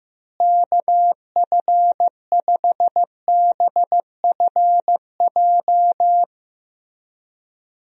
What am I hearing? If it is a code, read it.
Morse code, heard as a keyed tone, about 16 wpm, 700 Hz. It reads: KF5BFJ
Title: USA Call Signs - 6 Characters - Repeated 3x - 15wpm